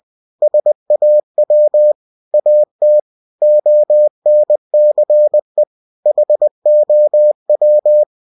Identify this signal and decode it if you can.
Morse code, heard as a keyed tone, about 20 wpm, 600 Hz. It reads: SAW AT ONCE HOW
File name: cw-books / winnie-the-pooh / winnie-20wpm-page-0082